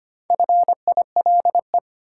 Morse code, heard as a keyed tone, about 25 wpm, 700 Hz. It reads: FILE
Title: Sentences from Top 1200 Words - New Word Drill Alt - 25wpm